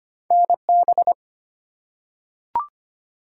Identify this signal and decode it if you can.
Morse code, heard as a keyed tone, about 25 wpm, 700 Hz. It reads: NB
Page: morse-code-ninja-practice-25wpm